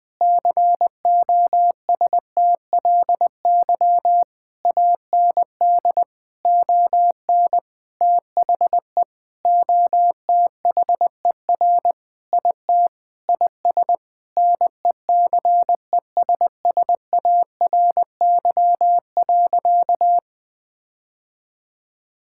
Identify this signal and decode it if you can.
Morse code, heard as a keyed tone, about 21 wpm, 700 Hz. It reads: COSTLY AND ON THE OTHER IT IS NECESSARY.
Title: Sentences with 8-Word Phrases - 20wpm